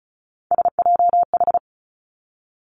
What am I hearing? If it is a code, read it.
Morse code, heard as a keyed tone, about 36 wpm, 700 Hz. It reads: SJH
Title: Three Challenging Letter-Number Combinations 35wpm